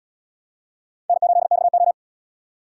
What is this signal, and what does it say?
Morse code, heard as a keyed tone, about 73 wpm, 700 Hz. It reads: N8XO